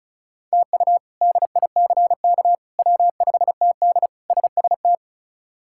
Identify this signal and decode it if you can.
Morse code, heard as a keyed tone, about 36 wpm, 700 Hz. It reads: TU DICK W5TD SST
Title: K1USN SST QSO Practice - 35wpm